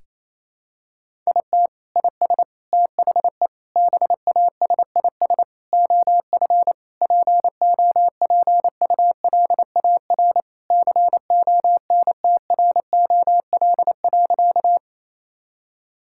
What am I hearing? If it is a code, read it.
Morse code, heard as a keyed tone, about 28 wpm, 700 Hz. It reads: IT IS THE BASIS OF POPULAR CONTROL.
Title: Sentences from Top 1000 Words - Review Part 5 - 28wpm